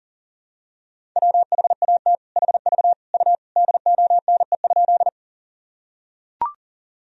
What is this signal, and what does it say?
Morse code, heard as a keyed tone, about 40 wpm, 700 Hz. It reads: WHAT HV U DONE?
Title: Sentences from Top 500 Words and Abbreviations 40wpm